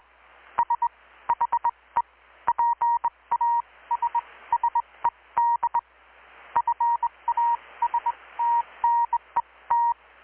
Morse code, heard as a keyed tone, about 20 wpm, spaced as for 17 wpm, 940 Hz. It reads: SHE PASSED FASTNET
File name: blog / swl / 6-abril-2019-12695-USB-KPS